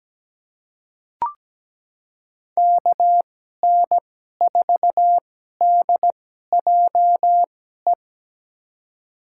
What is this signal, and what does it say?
Morse code, heard as a keyed tone, about 17 wpm, spaced as for 12 wpm, 700 Hz. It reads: KN4DJE
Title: USA Call Signs - 6 Characters - 2x Character Spacing - 17wpm